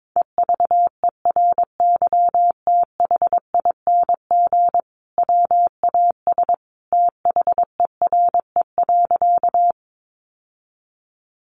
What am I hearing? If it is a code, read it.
Morse code, heard as a keyed tone, about 22 wpm, 700 Hz. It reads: EVERYTHING WAS THERE.
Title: Sentences from Top 700 Words 22wpm